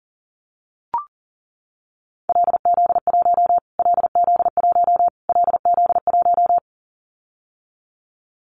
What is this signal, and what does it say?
Morse code, heard as a keyed tone, about 40 wpm, 700 Hz. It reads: LZ1 LZ1 LZ1
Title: Callsign Prefixes for Top 25 DXCC - Repeated 3x - 40wpm